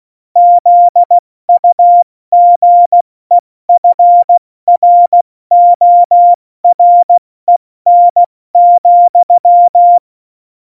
Morse code, heard as a keyed tone, about 16 wpm, 700 Hz. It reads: ZUGEFROREN,
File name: Morsekurs / Witze / Witz 05. Januar 2021, 70 F80